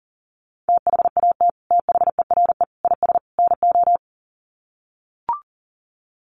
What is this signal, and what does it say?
Morse code, heard as a keyed tone, about 37 wpm, 700 Hz. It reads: THAT THERE IS NO
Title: Sets of 4 Words 40wpm